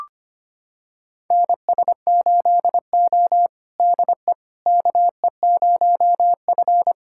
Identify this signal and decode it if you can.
Morse code, heard as a keyed tone, about 25 wpm, 700 Hz. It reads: NS8O DE KE0F
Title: USA Callsign DE Callsign - 4 Characters - 25wpm